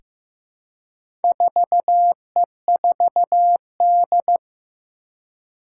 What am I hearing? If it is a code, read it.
Morse code, heard as a keyed tone, about 15 wpm, 700 Hz. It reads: 4E4D
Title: Four Challenging Letter-Number Combinations 15wpm